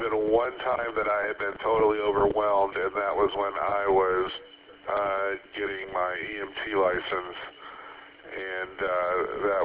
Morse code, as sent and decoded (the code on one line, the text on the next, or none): none